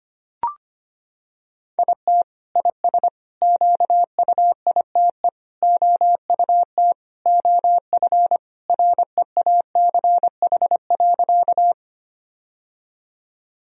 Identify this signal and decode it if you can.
Morse code, heard as a keyed tone, about 25 wpm, 700 Hz. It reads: IT IS QUITE OUT OF REACH.
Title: Sentences from Top 1000 Words - Review Part 2 - 25wpm